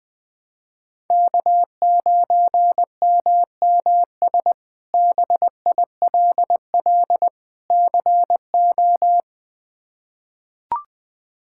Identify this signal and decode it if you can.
Morse code, heard as a keyed tone, about 20 wpm, 700 Hz. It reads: K9MMS BILL CO
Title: K1USN SST QSO Practice - 20wpm